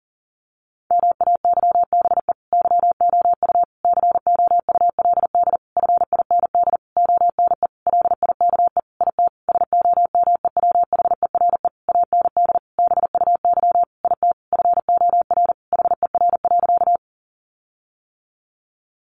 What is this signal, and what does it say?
Morse code, heard as a keyed tone, about 40 wpm, 700 Hz. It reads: MAYBE YOU COULD FIND ONE LIKE IT SOMEWHERE AND BUY IT FOR HER.
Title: Sentences from Top 1400 Words 40wpm